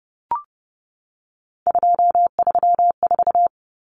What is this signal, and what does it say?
Morse code, heard as a keyed tone, about 30 wpm, 700 Hz. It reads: 234
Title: Numbers - 3-Digit 30wpm